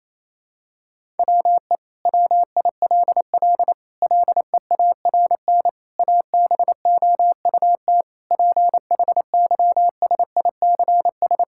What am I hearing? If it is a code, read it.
Morse code, heard as a keyed tone, about 28 wpm, 700 Hz. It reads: WE WILL LEARN ABOUT PHYSICS
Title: Sentences from Top 1000 Words - Review Part 5 - 28wpm